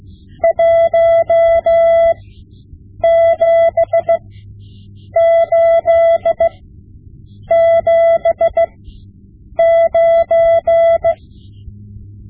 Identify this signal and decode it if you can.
Morse code, heard as a keyed tone, about 16 wpm, spaced as for 7 wpm, 670 Hz. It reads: 17879
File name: Morse Lessons / Morse Lessons / Tape-5B